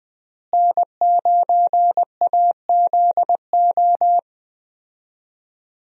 Morse code, heard as a keyed tone, about 20 wpm, 700 Hz. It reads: N9AZO
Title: USA Call Signs - 5 Characters - Repeated 2x - 20wpm